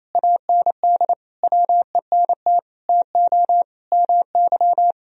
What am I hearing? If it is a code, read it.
Morse code, heard as a keyed tone, about 27 wpm, 700 Hz. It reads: AND WENT TO MY
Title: Sentences from Top 1000 Words - Review Part 3 - 28wpm